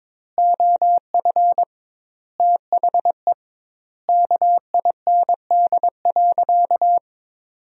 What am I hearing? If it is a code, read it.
Morse code, heard as a keyed tone, about 22 wpm, 700 Hz. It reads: OF THE KIND.
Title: 2x Word-Spacing - Sentences from Top 500 Words - 22wpm